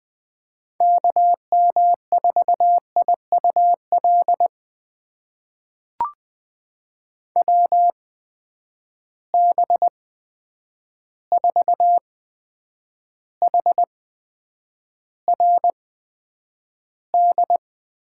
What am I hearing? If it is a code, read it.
Morse code, heard as a keyed tone, about 20 wpm, spaced as for 9 wpm, 700 Hz. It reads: KM4IUL WB4HRD
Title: USA Call Signs - 6 Characters - 8x Character Spacing - 20wpm